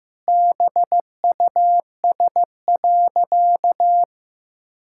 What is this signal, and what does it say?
Morse code, heard as a keyed tone, about 15 wpm, 700 Hz. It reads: BUS.
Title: Sentences from Top 1000 Words - Review Part 6 - 15wpm